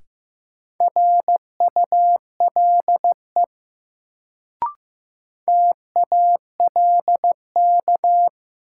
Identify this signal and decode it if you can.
Morse code, heard as a keyed tone, about 16 wpm, 700 Hz. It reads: RULE TALK
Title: Sentences from Top 400 Words - New Word Drill - 15wpm